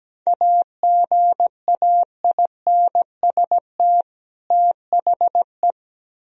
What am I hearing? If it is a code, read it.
Morse code, heard as a keyed tone, about 18 wpm, 700 Hz. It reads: AGAINST THE
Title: Speed-Racing - Sets of 2 Words 17wpm